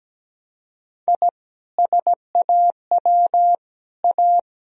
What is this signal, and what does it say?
Morse code, heard as a keyed tone, about 17 wpm, 700 Hz. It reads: I SAW A
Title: Sentences from Top 1000 Words - Review Part 2 - 17wpm